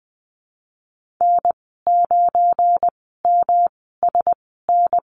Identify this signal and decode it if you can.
Morse code, heard as a keyed tone, about 20 wpm, spaced as for 15 wpm, 700 Hz. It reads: N9MSN
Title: 179 Lesson 29g - Slash - Call Signs 20wpm